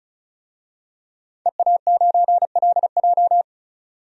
Morse code, heard as a keyed tone, about 34 wpm, 700 Hz. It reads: EA9LJ
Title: DXCC-100 Call Signs - 5 Characters - 35wpm